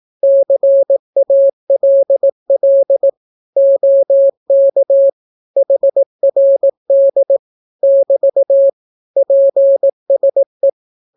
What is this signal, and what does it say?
Morse code, heard as a keyed tone, about 18 wpm, 550 Hz. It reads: CALL OK HRD <BT> PSE
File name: naukapodstaw / skroty / skroty